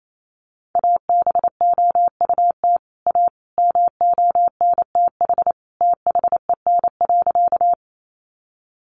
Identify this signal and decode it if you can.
Morse code, heard as a keyed tone, about 28 wpm, 700 Hz. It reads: ABOUT A MONTH THEN.
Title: Sentences from Top 800 Words 28wpm